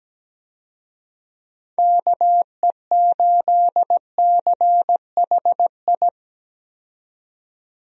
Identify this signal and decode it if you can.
Morse code, heard as a keyed tone, about 17 wpm, 700 Hz. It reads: KE8CHI
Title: USA Call Signs - 6 Characters - 17wpm